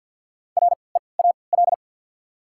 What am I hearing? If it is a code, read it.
Morse code, heard as a keyed tone, about 51 wpm, spaced as for 29 wpm, 700 Hz. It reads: REAL
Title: 3x Character Spacing - Top 500 Words - 50wpm